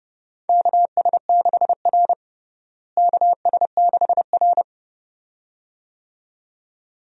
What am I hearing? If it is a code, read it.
Morse code, heard as a keyed tone, about 30 wpm, 700 Hz. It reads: KS6R KS6R
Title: USA Call Signs - 4 Characters - Repeated 2x - 30wpm